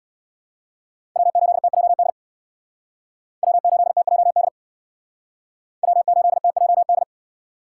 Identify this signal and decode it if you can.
Morse code, heard as a keyed tone, about 61 wpm, 700 Hz. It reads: W8TJD W8TJD W8TJD
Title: Speed Racing - USA 1x3 Call Signs 50wpm